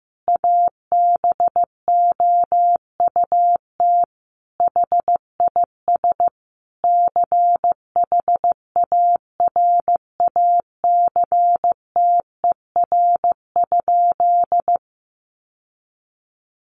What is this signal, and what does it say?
Morse code, heard as a keyed tone, about 15 wpm, 700 Hz. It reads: ABOUT HIS CHARACTER?
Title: Sentences from Top 1200 Words 15wpm